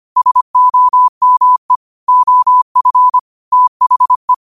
Morse code, heard as a keyed tone, about 25 wpm, 1000 Hz. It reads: SOME OF THE
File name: qotdfiles / 25wpm